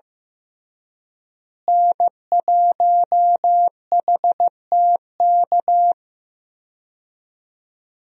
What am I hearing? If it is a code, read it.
Morse code, heard as a keyed tone, about 15 wpm, 700 Hz. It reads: N1HTK
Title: USA Call Signs - 5 Characters - Repeated 3x - 15wpm